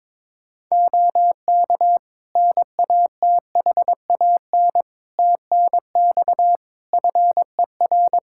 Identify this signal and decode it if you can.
Morse code, heard as a keyed tone, about 22 wpm, 700 Hz. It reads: OK NATHAN TNX FER
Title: Practice QSO - Exchange 4 - Ending - 22wpm